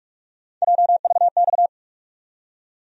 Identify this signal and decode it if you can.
Morse code, heard as a keyed tone, about 46 wpm, 700 Hz. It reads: JVX
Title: USA Call Sign Postfixes - 3 Letters 45wpm